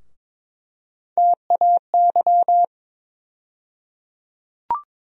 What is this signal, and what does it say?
Morse code, heard as a keyed tone, about 21 wpm, 700 Hz. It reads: TAY
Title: USA Call Sign Postfixes - 3 Letters 22wpm